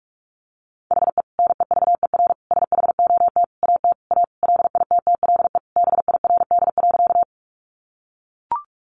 Morse code, heard as a keyed tone, about 44 wpm, 700 Hz. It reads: HE NEVER SHOT AT A LITTLE BIRD.
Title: Sentences from Top 1100 Words 45wpm